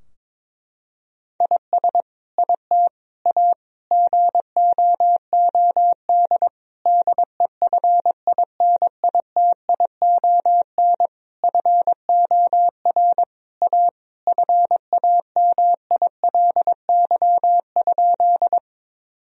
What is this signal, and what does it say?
Morse code, heard as a keyed tone, about 21 wpm, 700 Hz. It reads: IS IT A GOOD DEFINITION FOR A FAMILY?
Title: Sentences from Top 1000 Words - Review Part 4 - 22wpm